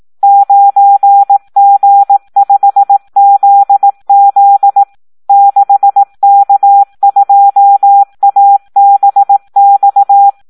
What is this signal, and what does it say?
Morse code, heard as a keyed tone, about 18 wpm, 800 Hz. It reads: 9G5ZZ 6K2ABX
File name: Audio practica CW / SD 1 18